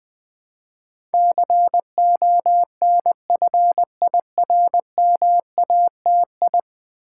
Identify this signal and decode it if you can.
Morse code, heard as a keyed tone, about 20 wpm, 700 Hz. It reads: CONFIRMATI
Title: Build A Word - 5 Letters At A Time - 12-Letter Words - 20wpm